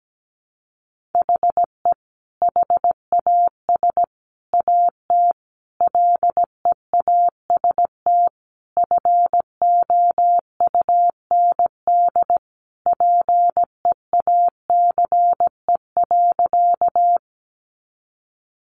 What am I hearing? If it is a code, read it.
Morse code, heard as a keyed tone, about 17 wpm, 700 Hz. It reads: HE HAS AT LEAST FOUND PEACE.
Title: Sentences from Top 1500 Words 17wpm